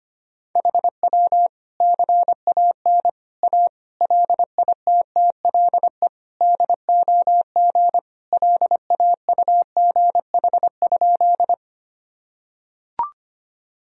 Morse code, heard as a keyed tone, about 25 wpm, 700 Hz. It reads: HW CAN A LITTLE DOG LAUGH?
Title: Sentences from Top 500 Words and Abbreviations 25wpm